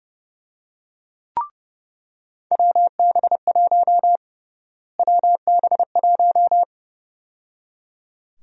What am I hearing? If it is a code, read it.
Morse code, heard as a keyed tone, about 30 wpm, 700 Hz. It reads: WB1 WB1
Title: USA Call Sign Prefixes - Repeated 2x - 30wpm